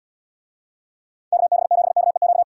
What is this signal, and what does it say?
Morse code, heard as a keyed tone, about 75 wpm, 700 Hz. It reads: KB7KEQ